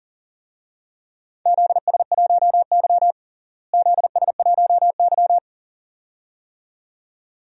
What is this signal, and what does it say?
Morse code, heard as a keyed tone, about 38 wpm, 700 Hz. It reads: ZS1Y ZS1Y
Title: DXCC-100 Call Signs - 4 Characters - Repeated 2x - 40wpm